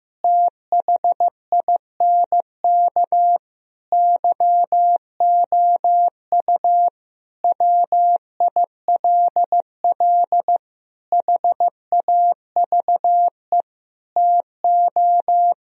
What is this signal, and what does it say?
Morse code, heard as a keyed tone, about 15 wpm, 700 Hz. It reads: THINK YOU WILL HAVE TO
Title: Sentences from Top 1000 Words - Review Part 4 - 15wpm